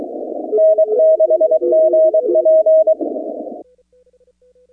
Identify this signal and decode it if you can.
Morse code, heard as a keyed tone, about 23 wpm, 630 Hz. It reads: N6GP
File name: contests / 2019-cw-wpx-kt5j-k5pi- -2330 0-25